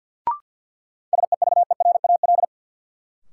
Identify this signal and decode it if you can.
Morse code, heard as a keyed tone, about 53 wpm, 700 Hz. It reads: SEVERAL
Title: Sentences from Top 400 Words - New Word Drill - 50wpm